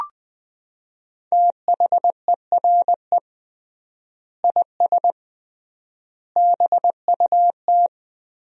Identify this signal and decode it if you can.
Morse code, heard as a keyed tone, about 20 wpm, 700 Hz. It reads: THERE IS BUT
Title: 3x Word-Spacing - Sentences from Top 100 Words - 20wpm